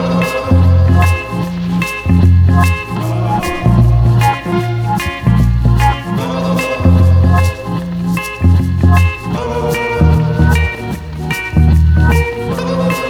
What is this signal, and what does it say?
There is no code.